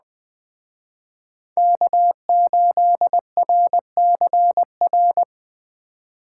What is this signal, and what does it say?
Morse code, heard as a keyed tone, about 20 wpm, 700 Hz. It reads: K8RCR
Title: Speed Racing - USA 1x3 Call Signs 17wpm